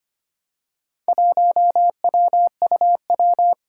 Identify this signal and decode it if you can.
Morse code, heard as a keyed tone, about 25 wpm, 700 Hz. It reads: 1WUW
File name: practice / 25wpm / Four Letter-Number Combinations - 25wpm